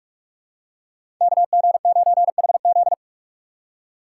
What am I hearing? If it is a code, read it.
Morse code, heard as a keyed tone, about 44 wpm, 700 Hz. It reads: KG9HZ
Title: USA Call Signs - 5 Characters - 4x Character Spacing - 45wpm